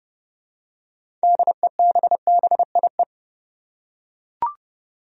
Morse code, heard as a keyed tone, about 30 wpm, 700 Hz. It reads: DEBBIE
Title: USA Common Names - 30wpm